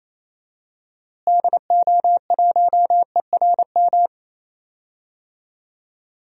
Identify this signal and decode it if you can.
Morse code, heard as a keyed tone, about 28 wpm, 700 Hz. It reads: DO1ERM
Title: DXCC-100 Call Signs - 6 Characters - Repeated 3x - 28wpm